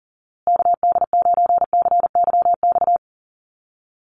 Morse code, heard as a keyed tone, about 40 wpm, 700 Hz. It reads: KD9CYX
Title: USA Call Signs Repeated 3x 40wpm